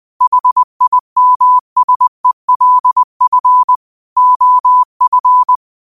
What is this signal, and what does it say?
Morse code, heard as a keyed tone, about 20 wpm, 1000 Hz. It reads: HIMSELF OF
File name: qotdfiles / 20wpm